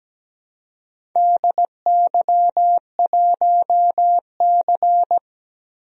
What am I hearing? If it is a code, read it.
Morse code, heard as a keyed tone, about 17 wpm, 700 Hz. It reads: DY1C
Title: DXCC-100 Call Signs - 4 Characters - 17wpm